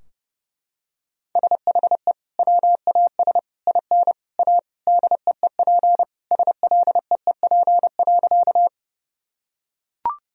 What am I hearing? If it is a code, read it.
Morse code, heard as a keyed tone, about 32 wpm, 700 Hz. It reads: SHE WAS IN A DEEP SLEEP.
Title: Sentences from Top 1000 Words - Review Part 3 - 30wpm